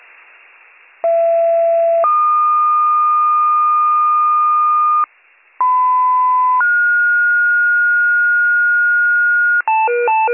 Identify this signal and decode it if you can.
Two-tone page, heard as a one-second tone, then a three-second tone, about 670 then 1220 Hz, about 1010 then 1430 Hz, with noise. It is heard from 1.0 to 9.6 s.